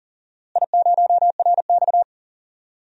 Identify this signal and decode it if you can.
Morse code, heard as a keyed tone, about 40 wpm, 700 Hz. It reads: I0RX
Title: DXCC-100 Call Signs - 4 Characters - Repeated 2x - 40wpm